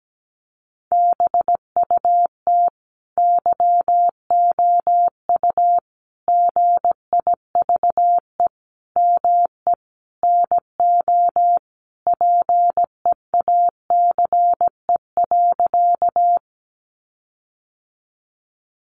Morse code, heard as a keyed tone, about 17 wpm, 700 Hz. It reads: BUT YOU GIVE ME NO PEACE.
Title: Sentences from Top 1500 Words 17wpm